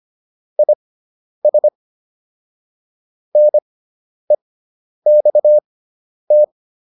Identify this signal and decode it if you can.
Morse code, heard as a keyed tone, about 25 wpm, spaced as for 10 wpm, 600 Hz. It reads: IS NEXT